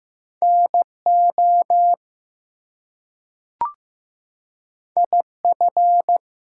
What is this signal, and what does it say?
Morse code, heard as a keyed tone, about 15 wpm, 700 Hz. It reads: NO IF
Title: Two Letter Words - 15wpm